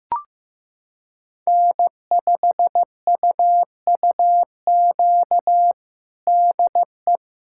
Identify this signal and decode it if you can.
Morse code, heard as a keyed tone, about 15 wpm, 700 Hz. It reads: N5UUQ DE
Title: USA Callsign DE Callsign 15wpm